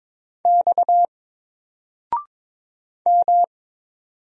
Morse code, heard as a keyed tone, about 22 wpm, 700 Hz. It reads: X M
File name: Single-Letter-Number-22wpm